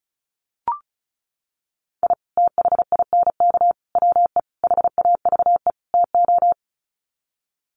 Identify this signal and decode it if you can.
Morse code, heard as a keyed tone, about 34 wpm, 700 Hz. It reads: I THINK WE HAVE TO
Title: Sets of 5 Words 35wpm